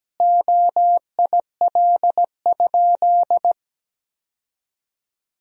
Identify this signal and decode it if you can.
Morse code, heard as a keyed tone, about 17 wpm, 700 Hz. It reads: OIL?
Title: Sentences from Top 1000 Words - Review Part 4 - 17wpm